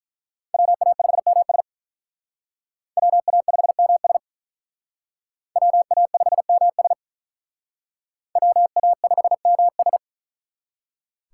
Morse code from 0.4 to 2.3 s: WA5MS

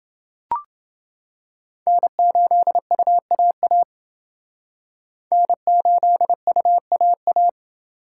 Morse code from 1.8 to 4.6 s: N8UAA

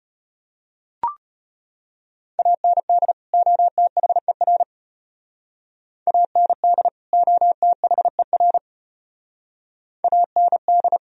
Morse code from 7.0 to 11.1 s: OTHER AND

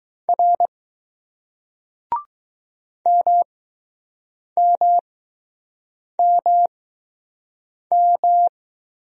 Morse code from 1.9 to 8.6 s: MMMM

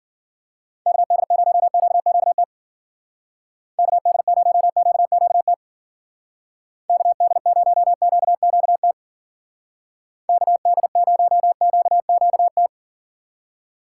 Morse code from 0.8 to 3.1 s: KD0QQT

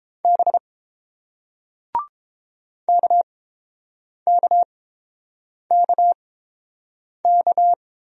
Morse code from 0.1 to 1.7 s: B

